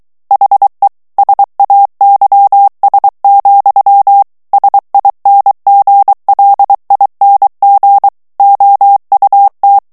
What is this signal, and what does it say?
Morse code, heard as a keyed tone, about 23 wpm, 800 Hz. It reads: HE SAYS, SINGLING OUT
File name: morse mp3 / ANGLAIS texte clair 2 vitesse 22 mots minute